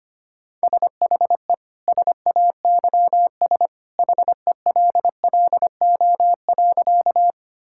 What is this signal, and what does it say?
Morse code, heard as a keyed tone, about 25 wpm, 700 Hz. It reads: SHE SAYS HELLO.
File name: practice / 25wpm / Sentences with 2-Word Phrases - 25wpm